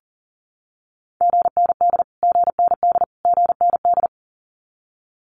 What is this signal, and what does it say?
Morse code, heard as a keyed tone, about 40 wpm, 700 Hz. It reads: GND GND GND
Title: Abbreviations Repeated 3x - 40wpm